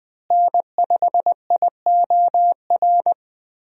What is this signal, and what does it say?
Morse code, heard as a keyed tone, about 21 wpm, 700 Hz. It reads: N5IOR
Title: USA Call Signs - 5 Characters - Repeated 3x - 20wpm